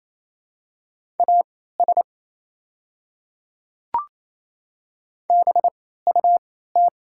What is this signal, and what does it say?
Morse code, heard as a keyed tone, about 27 wpm, spaced as for 16 wpm, 690 Hz. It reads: AS BUT
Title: 3x Character Spacing - Top 100 Words - 28wpm